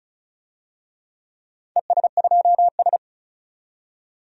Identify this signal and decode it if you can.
Morse code, heard as a keyed tone, about 34 wpm, 700 Hz. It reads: ES2S
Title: DXCC-100 Call Signs - 4 Characters - 35wpm